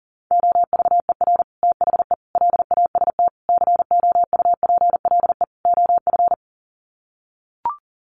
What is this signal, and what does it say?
Morse code, heard as a keyed tone, about 40 wpm, 700 Hz. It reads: OVER THE LAST COUPLE OF
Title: Sets of 5 Words Encore 40wpm